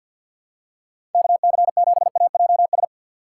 Morse code, heard as a keyed tone, about 52 wpm, 700 Hz. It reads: KC7AJS
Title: USA Call Signs - 3x Character Spacing - 50wpm